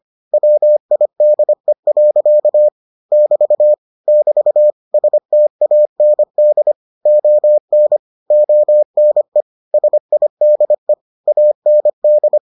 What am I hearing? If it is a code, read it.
Morse code, heard as a keyed tone, about 25 wpm, 600 Hz. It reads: WIDE. <BT> <BT> STAND ON ONE SIDE AND